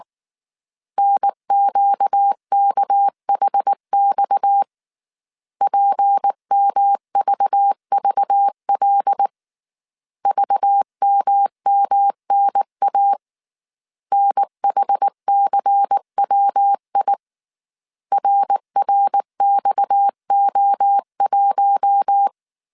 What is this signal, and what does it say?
Morse code, heard as a keyed tone, about 19 wpm, 800 Hz. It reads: NQKHX PMVVL VMMNA NHCWI RRXO1
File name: CW / DET / 1 6 1500C S A 18wpm 2255